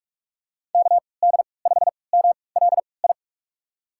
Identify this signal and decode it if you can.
Morse code, heard as a keyed tone, about 42 wpm, spaced as for 25 wpm, 700 Hz. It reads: KD5MLI
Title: USA Call Signs - 6 Characters - 3x Character Spacing - 45wpm